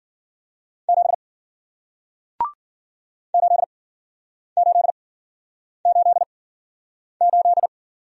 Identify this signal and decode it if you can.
Morse code, heard as a keyed tone, about 56 wpm, spaced as for 9 wpm, 700 Hz. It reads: 7 8888